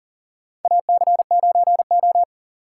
Morse code, heard as a keyed tone, about 40 wpm, 700 Hz. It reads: AC9O
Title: USA Call Signs - 4 Characters - 6x Character Spacing - 40wpm